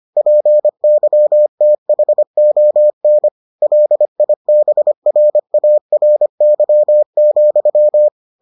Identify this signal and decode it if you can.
Morse code, heard as a keyed tone, about 25 wpm, 600 Hz. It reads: PYTHON LIBRARY,